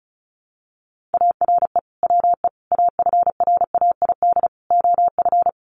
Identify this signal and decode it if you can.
Morse code, heard as a keyed tone, about 34 wpm, 700 Hz. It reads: ARE WE AFRAID OF
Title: Sentences from Top 1500 Words 35wpm